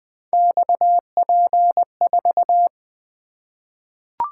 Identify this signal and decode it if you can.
Morse code, heard as a keyed tone, about 20 wpm, 700 Hz. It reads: XP4
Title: Three Letter-Number Combinations - 20wpm